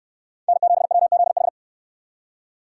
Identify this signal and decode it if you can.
Morse code, heard as a keyed tone, about 65 wpm, 700 Hz. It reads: N7WZH